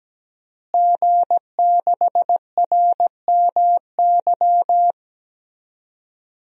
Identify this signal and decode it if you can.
Morse code, heard as a keyed tone, about 17 wpm, 700 Hz. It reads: G6RMY
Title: DXCC-100 Call Signs - 5 Characters - Repeated 2x - 17wpm